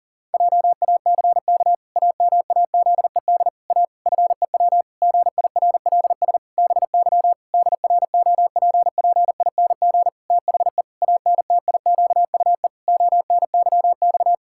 Morse code, heard as a keyed tone, about 37 wpm, 700 Hz. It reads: JACK AMAZED A FEW GIRLS BY DROPPING THE ANTIQUE ONYX
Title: Pangram Challenge v2 - 40wpm